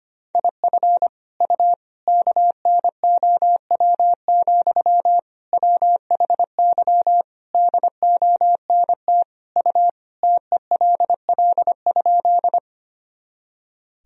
Morse code, heard as a keyed tone, about 25 wpm, 700 Hz. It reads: IF U KNOW, WHY DONT U TELL?
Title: Sentences from Top 300 Words and Abbreviations 25wpm